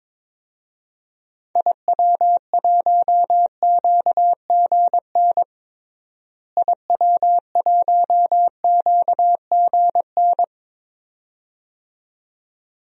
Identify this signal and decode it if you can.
Morse code, heard as a keyed tone, about 22 wpm, 700 Hz. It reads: IW1QGN IW1QGN
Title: DXCC-100 Call Signs - 6 Characters - Repeated 2x - 22wpm